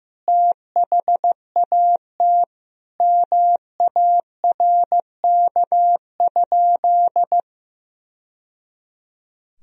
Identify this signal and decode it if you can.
Morse code, heard as a keyed tone, about 15 wpm, 700 Hz. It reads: THAT MARK?
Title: Sentences from Top 1000 Words - Review Part 1 - 15wpm